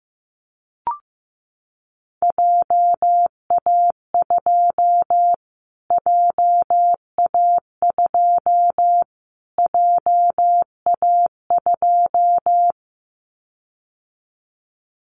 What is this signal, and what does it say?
Morse code, heard as a keyed tone, about 15 wpm, 700 Hz. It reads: JA2 JA2 JA2
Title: Callsign Prefixes for Top 25 DXCC - Repeated 3x - 15wpm